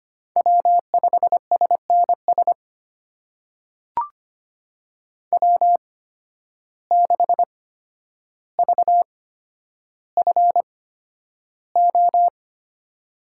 Morse code, heard as a keyed tone, about 25 wpm, spaced as for 11 wpm, 700 Hz. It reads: W5SNS W6VFO
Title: USA Call Signs - 5 Characters - 8x Character Spacing - 25wpm